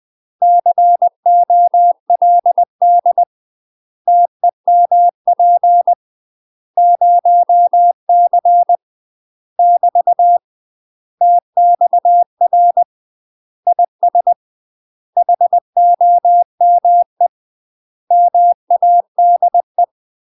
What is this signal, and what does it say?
Morse code, heard as a keyed tone, about 20 wpm, 700 Hz. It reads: COLD TEMP 0C <BT> TXR IS HOME MADE
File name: UK MASTER QSOs TRAINING / 20WPM / UK MASTER QSOs 0007